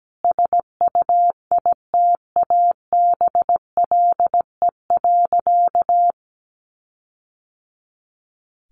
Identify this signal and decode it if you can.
Morse code, heard as a keyed tone, about 17 wpm, 700 Hz. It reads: SUITABLE.
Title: Sentences from Top 1400 Words 17wpm